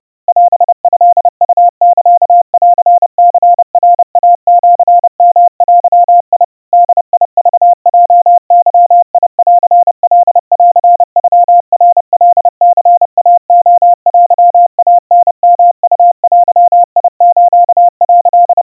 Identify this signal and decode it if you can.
Morse code (Japanese Wabun code), heard as a keyed tone, about 30 wpm, 700 Hz. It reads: オトウサンニナイシヨデ ボクヲゲンカンノナカニイレテイタヨウデス」
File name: kawa / Morse sample / kamyu 110 cpm 700 Hz Crispy